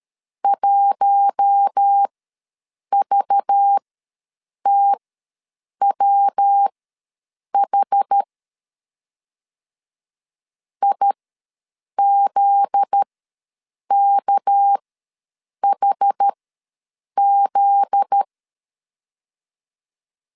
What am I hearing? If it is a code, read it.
Morse code, heard as a keyed tone, about 13 wpm, spaced as for 7 wpm, 800 Hz. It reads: 1VTWH IZKHZ
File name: LEZIONI / 7 GF9Z / 7 GF9Z ws 12wpm 3321